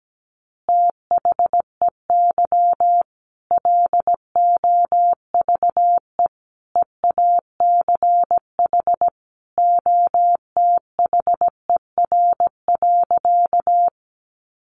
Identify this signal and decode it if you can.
Morse code, heard as a keyed tone, about 17 wpm, 700 Hz. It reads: THEY LOVE EACH OTHER.
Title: Easy Phrases 17wpm